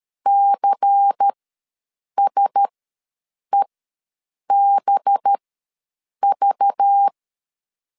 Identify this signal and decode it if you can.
Morse code, heard as a keyed tone, about 13 wpm, spaced as for 7 wpm, 800 Hz. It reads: CSEBV
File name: LEZIONI / 6 PKX0 / 6 PK ws 12wpm 3116